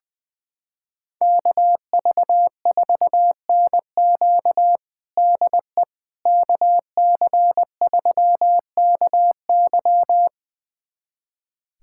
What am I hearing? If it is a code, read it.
Morse code, heard as a keyed tone, about 21 wpm, 700 Hz. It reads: KV4NQ DE KC3KY
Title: USA Callsign DE Callsign 20wpm